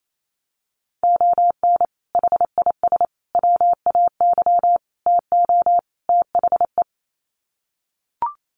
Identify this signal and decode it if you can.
Morse code, heard as a keyed tone, about 28 wpm, 700 Hz. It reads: ON HIS WAY TO THE
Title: Sets of 5 Words Encore2 28wpm